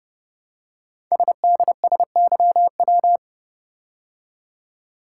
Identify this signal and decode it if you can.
Morse code, heard as a keyed tone, about 30 wpm, 700 Hz. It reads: SDSYW
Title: Code Groups - 5 Letter:Numbers - 30wpm